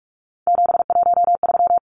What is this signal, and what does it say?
Morse code, heard as a keyed tone, about 44 wpm, 700 Hz. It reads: 713
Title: Numbers - 3-Digit 45wpm